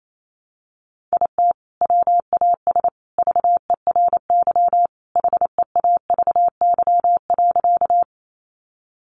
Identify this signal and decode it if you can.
Morse code, heard as a keyed tone, about 28 wpm, 700 Hz. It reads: IT WAS VERY HEAVY.
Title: Sentences from Top 1100 Words 28wpm